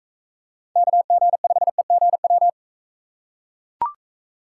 Morse code, heard as a keyed tone, about 39 wpm, 700 Hz. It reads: KG5EGW